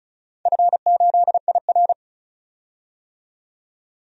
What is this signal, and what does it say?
Morse code, heard as a keyed tone, about 34 wpm, 700 Hz. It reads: F8IR
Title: DXCC-100 Call Signs - 4 Characters - 35wpm